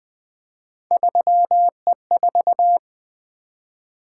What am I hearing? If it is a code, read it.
Morse code, heard as a keyed tone, about 20 wpm, 700 Hz. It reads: 3E4
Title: Three Letter-Number Combinations - 20wpm